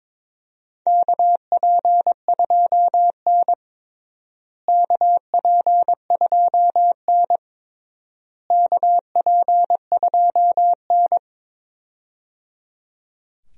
Morse code, heard as a keyed tone, about 22 wpm, 700 Hz. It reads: KP2N KP2N KP2N
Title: DXCC-100 Call Signs - 4 Characters - Repeated 3x - 22wpm